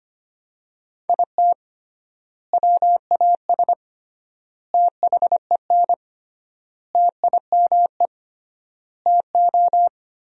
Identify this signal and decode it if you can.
Morse code, heard as a keyed tone, about 25 wpm, 700 Hz. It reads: IT WAS THEN TIME TO
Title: 3x Word-Spacing - Sentences from Top 300 Words - 25wpm